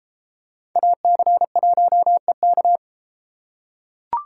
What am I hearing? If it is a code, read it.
Morse code, heard as a keyed tone, about 30 wpm, 700 Hz. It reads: AC1EK